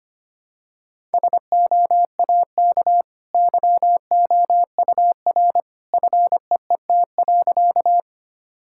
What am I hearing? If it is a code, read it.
Morse code, heard as a keyed tone, about 25 wpm, 700 Hz. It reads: SOAK YOUR FEET.
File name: Easy Sentences with Hints - 25wpm